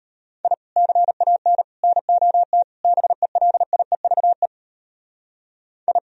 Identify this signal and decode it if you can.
Morse code, heard as a keyed tone, about 34 wpm, 700 Hz. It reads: I CAN NOT BELIEVE I